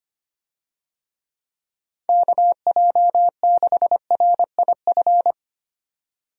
Morse code, heard as a keyed tone, about 25 wpm, 700 Hz. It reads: KJ6RIF
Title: USA Call Signs - 6 Characters - 25wpm